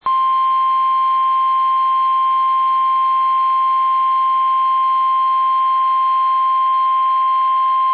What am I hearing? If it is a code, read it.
Emergency Alert System (SAME) tones (the 1050 Hz attention tone). It is heard from 0.5 to 7.0 s.
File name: eas / KUSO / KUSO